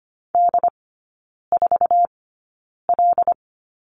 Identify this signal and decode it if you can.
Morse code, heard as a keyed tone, about 25 wpm, spaced as for 9 wpm, 700 Hz. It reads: D4L
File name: course / 25:10wpm / 100 Lesson 17e - Letter u - Three Characters 25wpm